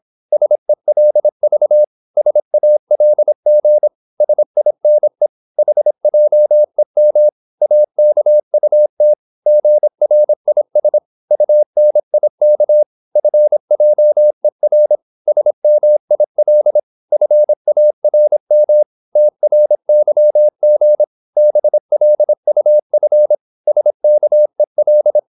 Morse code, heard as a keyed tone, about 26 wpm, 600 Hz. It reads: SELV SALG SINE HJEM AKUT GRIS UNIK FJER SMIL FARM TRYG BLUF SKEL
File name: Korte ord DK 26wpm